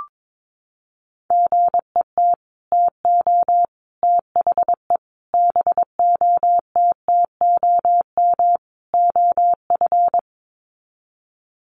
Morse code, heard as a keyed tone, about 22 wpm, 700 Hz. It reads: GET TO THE BOTTOM OF
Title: Sets of 5 Words Encore 22wpm